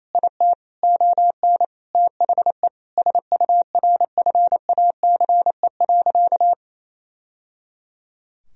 Morse code, heard as a keyed tone, about 28 wpm, 700 Hz. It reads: IT ON THE SURFACE.
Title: Sentences from Top 1000 Words - Review Part 3 - 28wpm